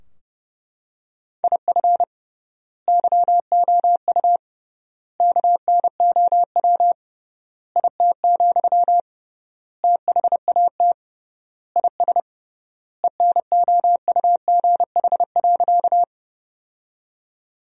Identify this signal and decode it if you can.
Morse code, heard as a keyed tone, about 29 wpm, 700 Hz. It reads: IF YOU KNOW IT, THAT IS ENOUGH.
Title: 3x Word-Spacing - Sentences from Top 400 Words - 30wpm